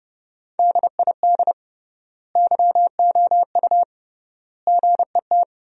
Morse code, heard as a keyed tone, about 30 wpm, 700 Hz. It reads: DID YOU GET
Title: 3x Word-Spacing - Sentences from Top 400 Words - 30wpm